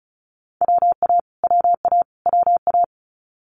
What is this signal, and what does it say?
Morse code, heard as a keyed tone, about 34 wpm, 700 Hz. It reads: WA WA WA
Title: US States Repeated 3x 35wpm